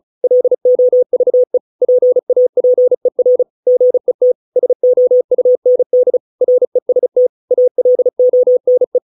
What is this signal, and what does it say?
Morse code, heard as a keyed tone, about 34 wpm, 500 Hz. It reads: LOVE PAPER GET SOUND REST ALONE